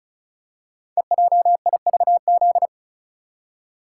Morse code, heard as a keyed tone, about 36 wpm, 700 Hz. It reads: EJIVZ
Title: Code Groups - 5 Letters - 35wpm